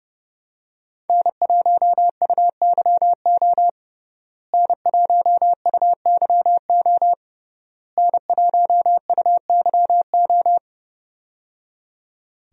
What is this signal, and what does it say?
Morse code, heard as a keyed tone, about 30 wpm, 700 Hz. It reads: N1UYO N1UYO N1UYO
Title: USA Call Signs - 5 Characters - Repeated 3x - 30wpm